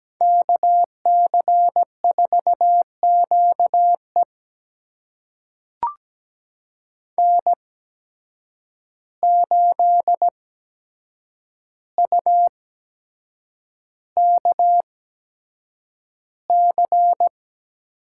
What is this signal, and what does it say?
Morse code, heard as a keyed tone, about 17 wpm, spaced as for 7 wpm, 700 Hz. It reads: KC4QE N8UKC
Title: USA Call Signs - 5 Characters - 8x Character Spacing - 17wpm